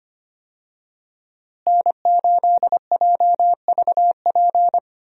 Morse code, heard as a keyed tone, about 25 wpm, 700 Hz. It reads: N8JVP
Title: USA Call Signs - 5 Characters - Repeated 2x - 25wpm